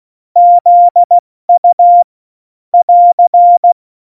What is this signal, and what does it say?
Morse code, heard as a keyed tone, about 16 wpm, 700 Hz. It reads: ZU <AR>
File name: Morsekurs / Witze / Witz 23. Juli 2019, 70 F80